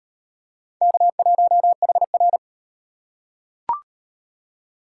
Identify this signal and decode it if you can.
Morse code, heard as a keyed tone, about 38 wpm, 700 Hz. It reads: K1HR